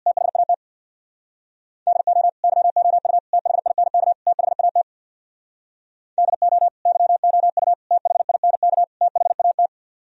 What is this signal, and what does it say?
Morse code, heard as a keyed tone, about 64 wpm, 700 Hz. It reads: THAT DO YOU THINK THAT DO YOU THINK THAT